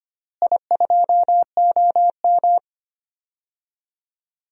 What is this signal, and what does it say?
Morse code, heard as a keyed tone, about 25 wpm, 700 Hz. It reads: I2OM
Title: DXCC-100 Call Signs - 4 Characters - Repeated 3x - 25wpm